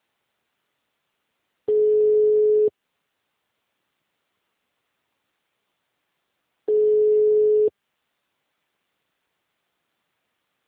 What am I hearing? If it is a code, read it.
Ringback tone (Europe, 425 Hz).